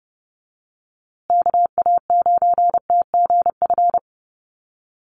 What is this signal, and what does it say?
Morse code, heard as a keyed tone, about 30 wpm, 700 Hz. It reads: KA9TGF
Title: USA Call Signs Repeated 2x 30wpm